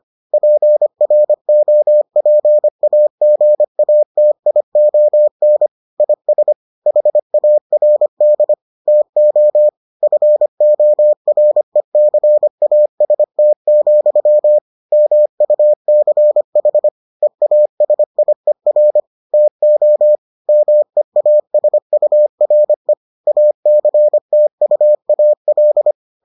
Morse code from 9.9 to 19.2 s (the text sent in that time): FORECAST, MUCH EASIER